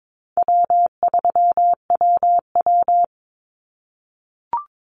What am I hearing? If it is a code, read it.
Morse code, heard as a keyed tone, about 22 wpm, 700 Hz. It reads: W3WW
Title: Call Signs - 22wpm